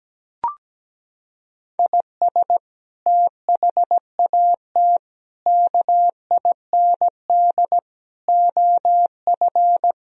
Morse code, heard as a keyed tone, about 17 wpm, 700 Hz. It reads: IS THAT KIND OF
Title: Sentences from Top 1000 Words - Review Part 3 - 17wpm